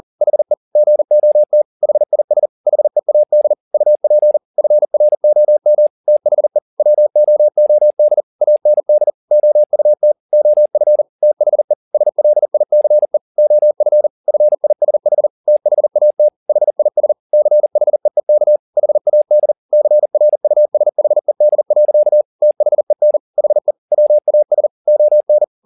Morse code, heard as a keyed tone, about 40 wpm, 600 Hz. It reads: HE GOT HIS HEAD UP FROM THE WOOD AND OUT OF THE SLICE OF FISH THAT HIS CHEEK HAD CRUSHED. THEN HE WAS ON
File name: the old man and the sea 2 40wpm